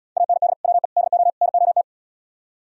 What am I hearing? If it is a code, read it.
Morse code, heard as a keyed tone, about 74 wpm, 700 Hz. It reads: AND WE DO NOT